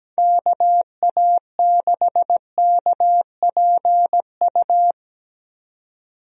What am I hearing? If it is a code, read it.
Morse code, heard as a keyed tone, about 17 wpm, 700 Hz. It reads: KA6KPU
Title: USA Call Signs - 6 Characters - 17wpm